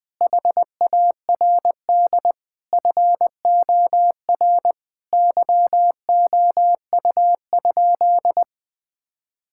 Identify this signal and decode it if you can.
Morse code, heard as a keyed tone, about 20 wpm, 700 Hz. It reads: HARD FOR YOU?
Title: Sentences from Top 1000 Words - Review Part 6 - 20wpm